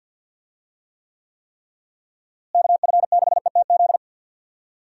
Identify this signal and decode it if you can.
Morse code, heard as a keyed tone, about 52 wpm, 700 Hz. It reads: KF6ETZ
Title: USA Call Signs - 6 Characters - Repeated 3x - 50wpm